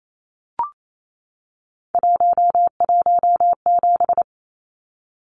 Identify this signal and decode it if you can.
Morse code, heard as a keyed tone, about 28 wpm, 700 Hz. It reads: 117
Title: Numbers - 3-Digit 28wpm